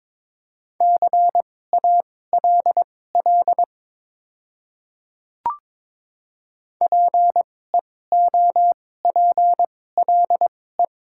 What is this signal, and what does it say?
Morse code, heard as a keyed tone, about 22 wpm, spaced as for 16 wpm, 700 Hz. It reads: CALL PEOPLE
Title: 2x Character Spacing - Top 100 Words - 22wpm